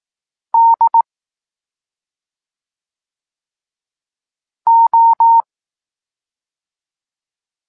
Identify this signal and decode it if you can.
Morse code, heard as a keyed tone, about 18 wpm, 920 Hz. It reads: D O
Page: Tekst 14 – doskonalenie odbioru poznanych liter